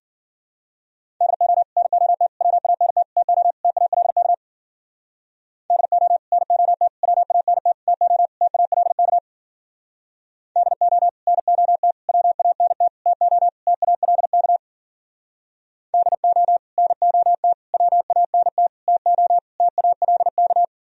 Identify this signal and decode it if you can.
Morse code, heard as a keyed tone, about 52 wpm, 700 Hz. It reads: DO NOT WANT TO TALK DO NOT WANT TO TALK DO NOT WANT TO TALK DO NOT WANT TO TALK